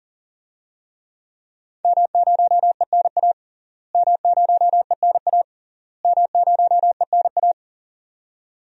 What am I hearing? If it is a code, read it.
Morse code, heard as a keyed tone, about 40 wpm, 700 Hz. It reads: M0ENA M0ENA M0ENA
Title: DXCC-100 Call Signs - 5 Characters - Repeated 3x - 40wpm